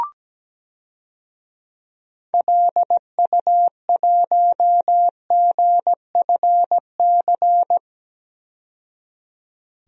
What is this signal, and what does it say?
Morse code, heard as a keyed tone, about 17 wpm, 700 Hz. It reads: LU1GFC
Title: DXCC-100 Call Signs - 6 Characters - 17wpm